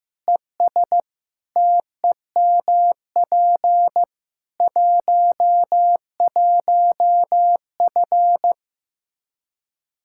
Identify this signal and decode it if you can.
Morse code, heard as a keyed tone, about 16 wpm, 700 Hz. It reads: ES TEMP 11F
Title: Practice QSO - Exchange 2 - Info - 15wpm